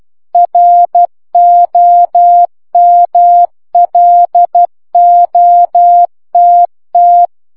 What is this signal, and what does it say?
Morse code, heard as a keyed tone, about 12 wpm, 700 Hz. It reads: ROMLOTT